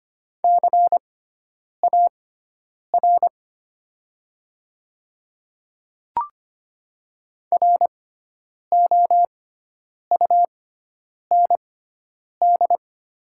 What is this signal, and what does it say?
Morse code, heard as a keyed tone, about 25 wpm, spaced as for 9 wpm, 700 Hz. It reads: CAR ROUND
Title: 6x Character Spacing - Top 300 Words - 25wpm